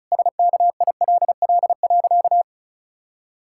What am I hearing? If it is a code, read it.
Morse code, heard as a keyed tone, about 36 wpm, 700 Hz. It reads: SKILL.
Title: Pangram Challenge v2 - 35wpm